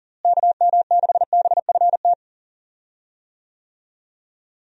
Morse code, heard as a keyed tone, about 37 wpm, 700 Hz. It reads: KM6BFT
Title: USA Call Signs - 6 Characters - Repeated 3x - 40wpm